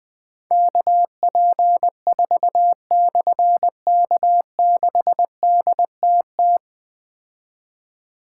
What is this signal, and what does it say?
Morse code, heard as a keyed tone, about 20 wpm, 700 Hz. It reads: KP4/K6DTT
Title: Difficult Call Signs 20wpm